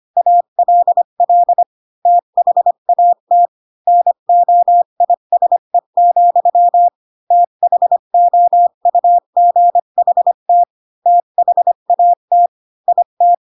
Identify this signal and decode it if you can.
Morse code, heard as a keyed tone, about 25 wpm, 700 Hz. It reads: ALL THAT NOISE, THOUGHT THAT IT
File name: Pinocchio / Pinocchio ch3 20 25 ik1hge